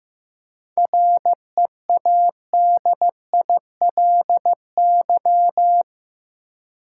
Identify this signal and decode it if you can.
Morse code, heard as a keyed tone, about 15 wpm, 700 Hz. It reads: READILY
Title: Seven Letter Words - 15wpm